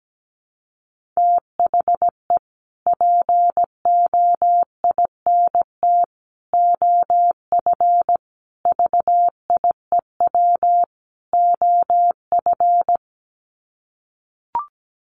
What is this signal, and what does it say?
Morse code, heard as a keyed tone, about 17 wpm, 700 Hz. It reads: THE POINT OF VIEW OF
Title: Sets of 5 Words 17wpm